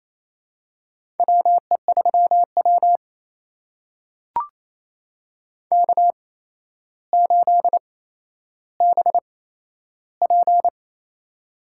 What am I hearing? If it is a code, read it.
Morse code, heard as a keyed tone, about 28 wpm, spaced as for 12 wpm, 700 Hz. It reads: WE3W K8BP
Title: USA Call Signs - 4 Characters - 8x Character Spacing - 28wpm